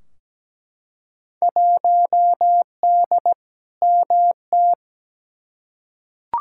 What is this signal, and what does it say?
Morse code, heard as a keyed tone, about 17 wpm, 700 Hz. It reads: 1D MT
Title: Field Day Exchanges - 17wpm